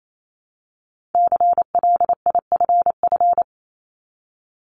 Morse code, heard as a keyed tone, about 28 wpm, 700 Hz. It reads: CLIFF
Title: Multi-letter Phonemes 28wpm